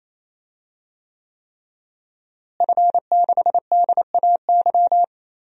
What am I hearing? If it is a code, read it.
Morse code, heard as a keyed tone, about 28 wpm, 700 Hz. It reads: F6DAY
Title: DXCC-100 Call Signs - 5 Characters - Repeated 3x - 28wpm